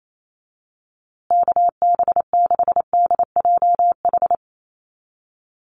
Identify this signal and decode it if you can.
Morse code, heard as a keyed tone, about 28 wpm, 700 Hz. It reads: KB6DJH
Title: USA Call Signs Repeated 3x 28wpm